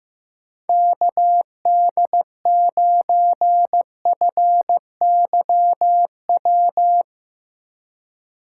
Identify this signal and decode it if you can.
Morse code, heard as a keyed tone, about 15 wpm, 700 Hz. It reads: KD9FYW
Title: USA Call Signs - 2x Character Spacing - 15wpm